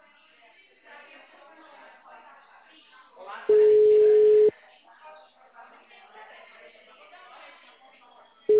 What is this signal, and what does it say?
Ringback tone (Europe, 425 Hz), with speech.
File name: out- -2025- - - .4887